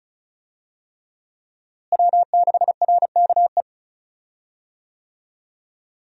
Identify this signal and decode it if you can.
Morse code, heard as a keyed tone, about 36 wpm, 700 Hz. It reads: W6RKE
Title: USA Call Signs - 5 Characters - Repeated 3x - 35wpm